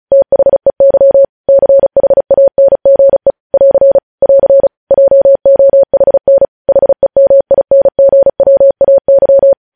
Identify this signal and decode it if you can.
Morse code, heard as a keyed tone, about 36 wpm, 550 Hz. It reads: THEY CHANGE <AR> <AR> JOHN HEMINGWAY
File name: podcast / News headlines-1x-35x35